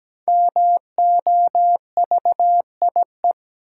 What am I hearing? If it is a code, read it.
Morse code, heard as a keyed tone, about 17 wpm, 700 Hz. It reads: MOVIE
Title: Five Letter Words - 17wpm